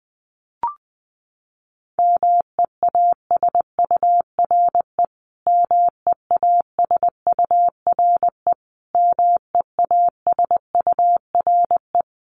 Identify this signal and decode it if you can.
Morse code, heard as a keyed tone, about 19 wpm, 700 Hz. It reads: MEASURE MEASURE MEASURE
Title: Top 500 Words Repeated 20wpm